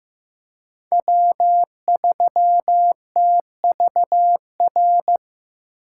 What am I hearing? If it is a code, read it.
Morse code, heard as a keyed tone, about 15 wpm, 700 Hz. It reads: W3TVR
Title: USA Call Signs - 5 Characters - 4x Character Spacing - 15wpm